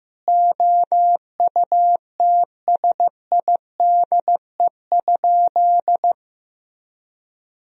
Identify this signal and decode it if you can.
Morse code, heard as a keyed tone, about 16 wpm, 700 Hz. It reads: OUTSIDE?
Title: Sentences from Top 1000 Words - Review Part 4 - 15wpm